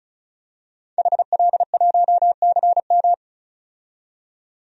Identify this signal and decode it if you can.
Morse code, heard as a keyed tone, about 36 wpm, 700 Hz. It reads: HL1CM
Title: DXCC-100 Call Signs - 5 Characters - 35wpm